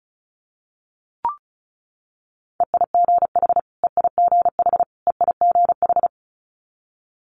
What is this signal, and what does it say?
Morse code, heard as a keyed tone, about 36 wpm, 700 Hz. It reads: EIGH EIGH EIGH
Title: Multi-letter Phonemes Repeated 3x 35wpm